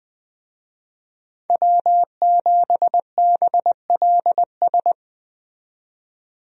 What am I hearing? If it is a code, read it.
Morse code, heard as a keyed tone, about 20 wpm, 700 Hz. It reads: W7BLS
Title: USA Call Signs - 5 Characters - 20wpm